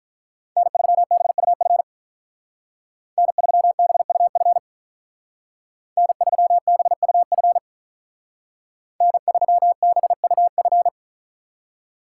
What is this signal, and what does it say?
Morse code, heard as a keyed tone, about 43 wpm, 700 Hz. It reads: N3BUF N3BUF N3BUF N3BUF
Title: Speed Racing - USA Call Signs 35wpm